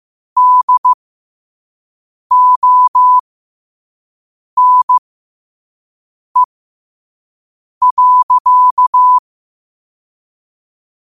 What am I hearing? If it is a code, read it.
Morse code, heard as a keyed tone, about 15 wpm, spaced as for 5 wpm, 1000 Hz. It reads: DONE.
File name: qotdfiles / 5wpm